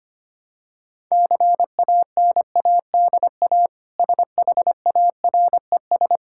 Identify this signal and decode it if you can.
Morse code, heard as a keyed tone, about 25 wpm, 700 Hz. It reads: CANADA SHARES
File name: practice / 25wpm / Sentences with 4-Word Phrases - 25wpm